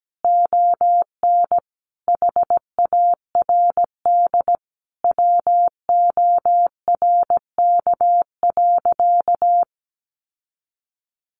Morse code, heard as a keyed tone, about 17 wpm, 700 Hz. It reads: ON HARD WORK.
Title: Sentences from Top 1400 Words 17wpm